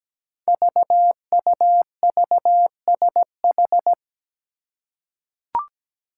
Morse code, heard as a keyed tone, about 17 wpm, 700 Hz. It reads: VUVSH
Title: Five Challenging Letter-Number Combinations 17wpm